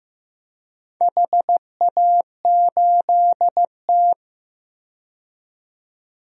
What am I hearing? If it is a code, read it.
Morse code, heard as a keyed tone, about 15 wpm, 700 Hz. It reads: HA8T
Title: DXCC-100 Call Signs - 4 Characters - 15wpm